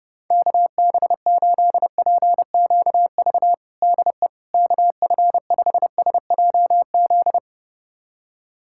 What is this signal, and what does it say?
Morse code, heard as a keyed tone, about 30 wpm, 700 Hz. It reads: KB8PQV DE KF5SJZ
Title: USA Callsign DE Callsign - 6 Characters - 30wpm